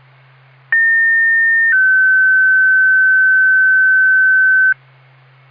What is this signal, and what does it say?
Two-tone page, heard as a one-second tone, then a three-second tone, about 1800 then 1550 Hz. It is heard from 0.7 to 4.7 s.